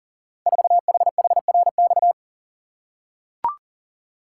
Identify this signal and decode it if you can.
Morse code, heard as a keyed tone, about 38 wpm, 700 Hz. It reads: 4HHRX
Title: Code Groups - 5 Letter:Numbers - 40wpm